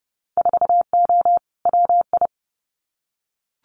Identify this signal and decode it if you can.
Morse code, heard as a keyed tone, about 31 wpm, 700 Hz. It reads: 4O WI
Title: Winter Field Day Exchanges 30wpm